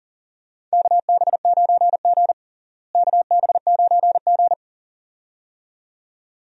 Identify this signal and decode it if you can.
Morse code, heard as a keyed tone, about 37 wpm, 700 Hz. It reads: KB9G KB9G
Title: USA Call Signs - 4 Characters - Repeated 2x - 40wpm